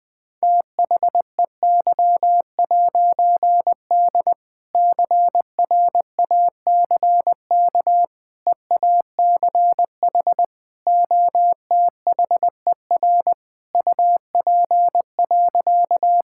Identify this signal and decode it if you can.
Morse code, heard as a keyed tone, about 21 wpm, 700 Hz. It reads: THEY'D CRACK EACH OTHER UP.
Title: All-Age Appropriate Jokes - 20wpm